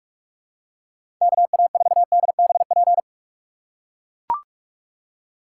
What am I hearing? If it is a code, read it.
Morse code, heard as a keyed tone, about 42 wpm, 700 Hz. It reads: KA4DBP
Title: USA Call Signs - 6 Characters - 2x Character Spacing - 45wpm